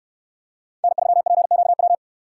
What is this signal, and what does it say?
Morse code, heard as a keyed tone, about 66 wpm, 700 Hz. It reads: N3PQW